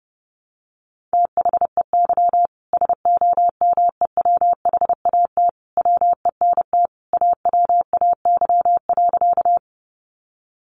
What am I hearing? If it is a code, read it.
Morse code, heard as a keyed tone, about 30 wpm, 700 Hz. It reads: THEY SOMEWHAT WENT AWAY.
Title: Sentences from Top 1300 Words 30wpm